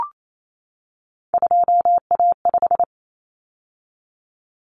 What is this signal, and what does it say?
Morse code, heard as a keyed tone, about 28 wpm, 700 Hz. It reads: 2A5
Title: Common Cut Numbers - 3-Digit 28wpm